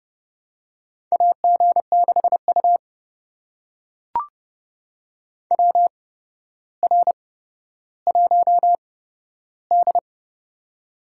Morse code, heard as a keyed tone, about 30 wpm, spaced as for 13 wpm, 700 Hz. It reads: AG6U WR1D
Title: USA Call Signs - 4 Characters - 8x Character Spacing - 30wpm